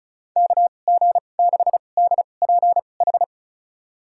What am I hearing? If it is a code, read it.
Morse code, heard as a keyed tone, about 36 wpm, spaced as for 25 wpm, 700 Hz. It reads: KG6DPH
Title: USA Call Signs - 6 Characters - 2x Character Spacing - 35wpm